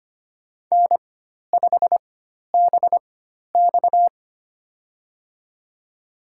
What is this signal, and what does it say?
Morse code, heard as a keyed tone, about 25 wpm, spaced as for 12 wpm, 700 Hz. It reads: N5BX
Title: USA Call Signs - 4x Character Spacing - 25wpm